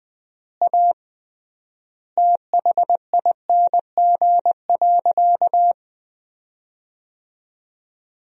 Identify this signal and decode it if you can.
Morse code, heard as a keyed tone, about 21 wpm, 700 Hz. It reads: A THING.
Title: 3x Word-Spacing - Sentences from Top 200 Words - 20wpm